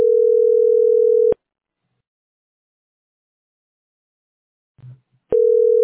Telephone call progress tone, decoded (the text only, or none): ringback tone